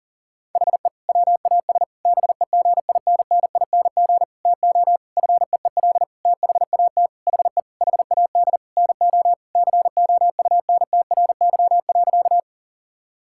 Morse code, heard as a keyed tone, about 40 wpm, 700 Hz. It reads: HE WAS BEGINNING TO FEEL THAT HE HAD NO COUNTRY.
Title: Sentences from Top 1000 Words - Review Part 5 - 40wpm